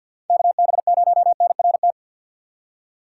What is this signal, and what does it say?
Morse code, heard as a keyed tone, about 50 wpm, 700 Hz. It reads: KB0NRT
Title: USA Call Signs - 6 Characters - 3x Character Spacing - 50wpm